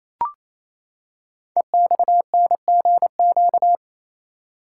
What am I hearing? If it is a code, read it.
Morse code, heard as a keyed tone, about 29 wpm, 700 Hz. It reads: EXNGQ
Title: Code Groups - 5 Letters - 28wpm